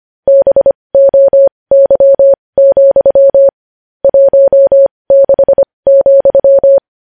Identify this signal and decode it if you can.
Morse code, heard as a keyed tone, about 25 wpm, spaced as for 20 wpm, 550 Hz. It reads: BOY, 16,
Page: news-headlines-in-morse-code-at-20-wpm